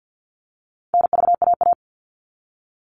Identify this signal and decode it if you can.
Morse code, heard as a keyed tone, about 50 wpm, 700 Hz. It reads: NVAA
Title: All Cut Numbers - 4-Digit 50wpm